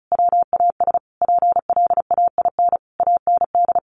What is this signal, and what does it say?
Morse code, heard as a keyed tone, about 34 wpm, 700 Hz. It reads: WAS PLAIN AND
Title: Sentences from Top 1400 Words 35wpm